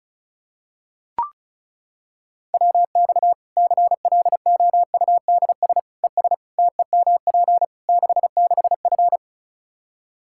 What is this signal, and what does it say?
Morse code, heard as a keyed tone, about 34 wpm, 700 Hz. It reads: WX CLOUDS ES TEMP 66F
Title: Practice QSO - Exchange 2 - Info - 35wpm